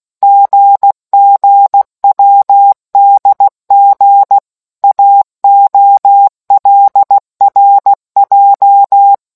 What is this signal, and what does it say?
Morse code, heard as a keyed tone, about 16 wpm, 800 Hz. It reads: GGWDG AOLRJ